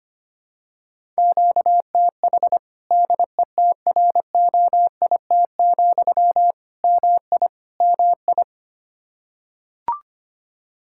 Morse code, heard as a keyed tone, about 25 wpm, 700 Hz. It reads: QTH DETROIT, MI MI
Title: Practice QSO - Exchange 1 - Beginning - 25wpm